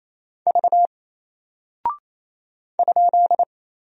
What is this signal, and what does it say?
Morse code, heard as a keyed tone, about 27 wpm, 700 Hz. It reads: V ?
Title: Single Character (Letter:Number:Punctuation:Prosign) 28wpm